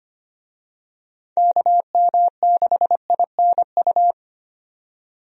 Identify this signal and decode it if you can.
Morse code, heard as a keyed tone, about 25 wpm, 700 Hz. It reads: KM6INU
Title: USA Call Signs - 6 Characters - 25wpm